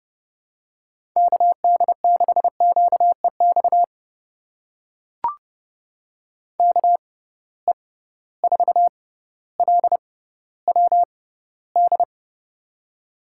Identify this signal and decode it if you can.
Morse code, heard as a keyed tone, about 30 wpm, spaced as for 15 wpm, 700 Hz. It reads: KD6QEX KE4LWD
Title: USA Call Signs - 6 Characters - 6x Character Spacing - 30wpm